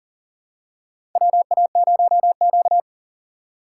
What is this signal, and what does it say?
Morse code, heard as a keyed tone, about 38 wpm, 700 Hz. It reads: WA0Q
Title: USA Call Signs - 4 Characters - 4x Character Spacing - 40wpm